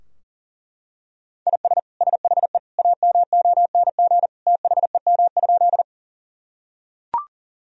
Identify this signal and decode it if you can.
Morse code, heard as a keyed tone, about 40 wpm, 700 Hz. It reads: IS SHE AMONG THEM?
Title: Sentences from Top 1000 Words - Review Part 3 - 40wpm